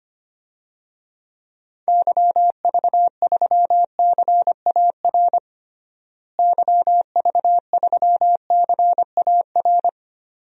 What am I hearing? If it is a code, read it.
Morse code, heard as a keyed tone, about 25 wpm, 700 Hz. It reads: YV3CAR YV3CAR
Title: DXCC-100 Call Signs - 6 Characters - Repeated 2x - 25wpm